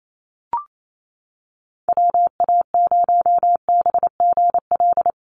Morse code, heard as a keyed tone, about 28 wpm, 700 Hz. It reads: WA0BGL
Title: USA Call Signs Repeated 3x 28wpm